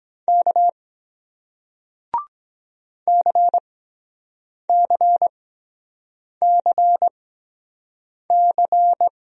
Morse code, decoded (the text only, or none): K CCCC